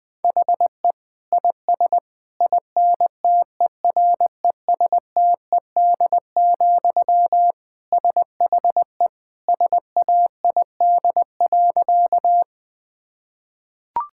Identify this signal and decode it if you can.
Morse code, heard as a keyed tone, about 20 wpm, 700 Hz. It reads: HE IS INTERESTED, SHE SAID.
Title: Sentences from Top 1000 Words - Review Part 6 - 20wpm